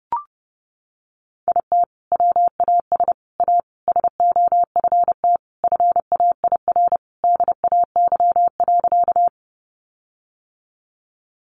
Morse code, heard as a keyed tone, about 30 wpm, 700 Hz. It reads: IT WAS A SOFT FAIR DAY.
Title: Sentences from Top 1500 Words 30wpm